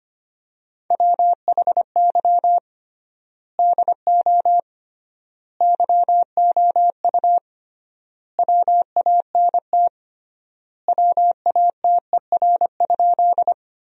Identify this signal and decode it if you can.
Morse code, heard as a keyed tone, about 25 wpm, 700 Hz. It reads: WHY DO YOU WANT WATER?
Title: 3x Word-Spacing - Sentences from Top 200 Words - 25wpm